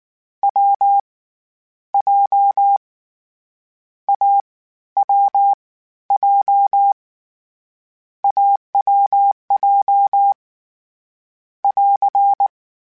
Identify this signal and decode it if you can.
Morse code, heard as a keyed tone, about 19 wpm, 800 Hz. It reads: W J A W J AWJ <AR>
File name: Difficult Combos / Difficult Combos-mp3s / AWJ @18